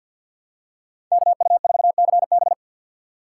Morse code, heard as a keyed tone, about 51 wpm, 700 Hz. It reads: KU4CB
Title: USA Call Signs - 3x Character Spacing - 50wpm